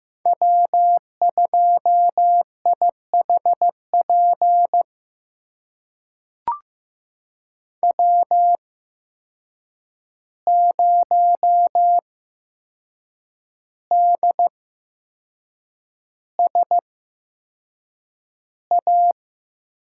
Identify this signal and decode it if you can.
Morse code, heard as a keyed tone, about 15 wpm, spaced as for 6 wpm, 700 Hz. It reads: W2IHP W0DSA
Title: USA Call Signs - 5 Characters - 8x Character Spacing - 15wpm